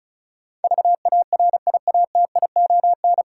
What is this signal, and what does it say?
Morse code, heard as a keyed tone, about 36 wpm, 700 Hz. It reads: VARIATION
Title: Sentences from Top 1600 Words - New Word Drill - 35wpm